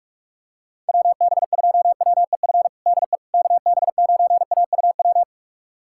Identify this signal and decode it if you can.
Morse code, heard as a keyed tone, about 41 wpm, 700 Hz. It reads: WB2WEF DE KB9AUW
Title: USA Callsign DE Callsign 45wpm